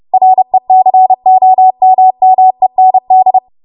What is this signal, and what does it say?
Morse code, heard as a keyed tone, about 30 wpm, 750 Hz. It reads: RECOMMEND
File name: Morse / 30WPM